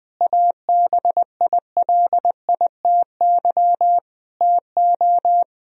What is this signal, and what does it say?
Morse code, heard as a keyed tone, about 20 wpm, 700 Hz. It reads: ABILITY TO
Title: Sentences from Top 1000 Words - Review Part 3 - 20wpm